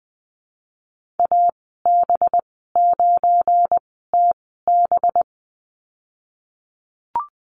Morse code, heard as a keyed tone, about 20 wpm, spaced as for 14 wpm, 700 Hz. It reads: AB9TB
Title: 238 Lesson 38g - Letter q - Call Signs 20wpm